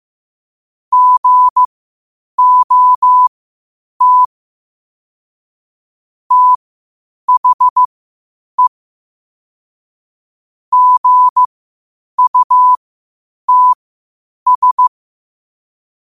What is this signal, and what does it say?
Morse code, heard as a keyed tone, about 15 wpm, spaced as for 8 wpm, 1000 Hz. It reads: GOT THE GUTS
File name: qotdfiles / 8wpm